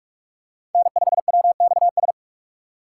Morse code, heard as a keyed tone, about 45 wpm, 690 Hz. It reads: N5WXS
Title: USA Call Signs - 5 Characters - 2x Character Spacing - 45wpm